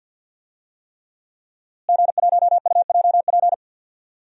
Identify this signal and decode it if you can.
Morse code, heard as a keyed tone, about 46 wpm, 700 Hz. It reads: G1UJP
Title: DXCC-100 Call Signs - 5 Characters - Repeated 3x - 50wpm